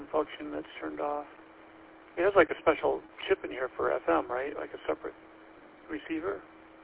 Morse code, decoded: none

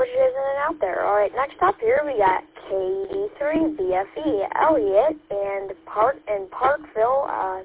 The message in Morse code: none

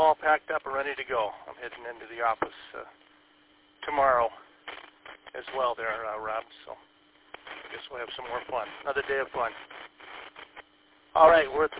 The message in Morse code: none

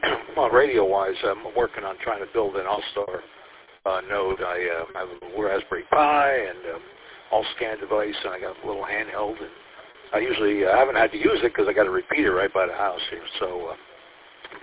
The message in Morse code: none

